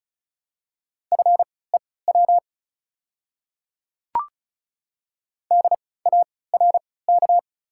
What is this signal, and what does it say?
Morse code, heard as a keyed tone, about 34 wpm, spaced as for 20 wpm, 690 Hz. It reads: FEW DARK
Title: 3x Character Spacing - Top 500 Words - 35wpm